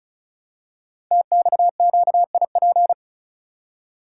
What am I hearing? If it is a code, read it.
Morse code, heard as a keyed tone, about 36 wpm, 700 Hz. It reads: TXQIP
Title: Code Groups - 5 Letters - 35wpm